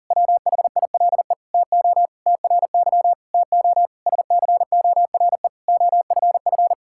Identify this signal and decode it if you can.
Morse code, heard as a keyed tone, about 40 wpm, 700 Hz. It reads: WHILE TO TRY TO SCORE OFF
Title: Sentences from Top 1600 Words 40wpm